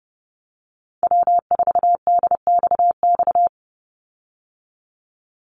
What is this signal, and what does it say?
Morse code, heard as a keyed tone, about 30 wpm, 700 Hz. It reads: W4DXX
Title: Call Signs Repeated 2x - 30wpm